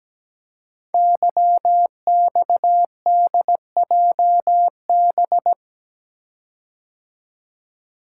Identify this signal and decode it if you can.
Morse code, heard as a keyed tone, about 17 wpm, 700 Hz. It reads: YXDJB
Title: Code Groups - 5 Letters - 17wpm